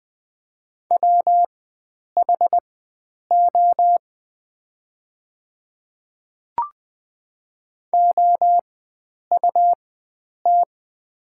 Morse code, heard as a keyed tone, about 20 wpm, spaced as for 9 wpm, 700 Hz. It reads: WHO OUT
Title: 4x Character Spacing - Top 100 Words - 20wpm